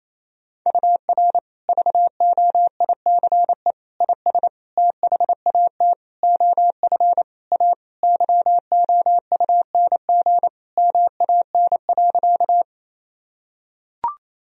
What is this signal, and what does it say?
Morse code, heard as a keyed tone, about 28 wpm, 700 Hz. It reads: UR VOICE IS THAT OF A YOUNG MAN.
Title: Sentences from Top 500 Words and Abbreviations 28wpm